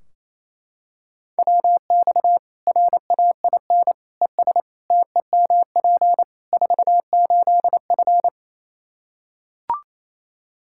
Morse code, heard as a keyed tone, about 28 wpm, 700 Hz. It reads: WX RAIN ES TEMP 48F
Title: Practice QSO - Exchange 2 - Info - 28wpm